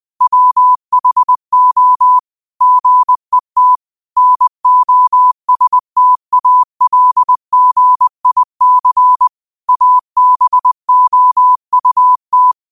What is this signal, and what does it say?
Morse code, heard as a keyed tone, about 20 wpm, 1000 Hz. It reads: WHO GET NOSTALGIC ABOUT